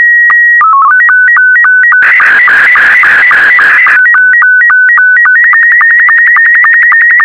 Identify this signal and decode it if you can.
SSTV (slow-scan television), Scottie 2.